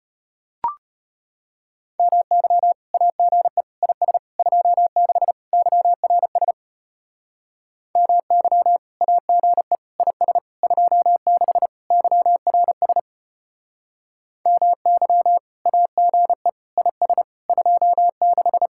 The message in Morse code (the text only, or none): MY AGE IS 26 YRS MY AGE IS 26 YRS MY AGE IS 26